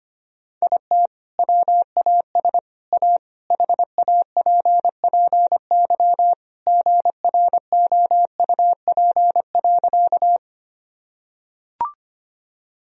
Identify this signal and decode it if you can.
Morse code, heard as a keyed tone, about 25 wpm, 700 Hz. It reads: IT WAS A HAPPY GROUP.
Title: Sentences from Top 1000 Words - Review Part 1 - 25wpm